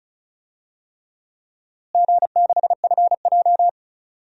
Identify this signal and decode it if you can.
Morse code, heard as a keyed tone, about 34 wpm, 700 Hz. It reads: G6FJ
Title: DXCC-100 Call Signs - 4 Characters - Repeated 3x - 35wpm